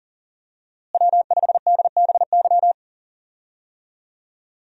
Morse code, heard as a keyed tone, about 40 wpm, 700 Hz. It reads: W5DBY
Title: USA Call Signs - 5 Characters - Repeated 2x - 40wpm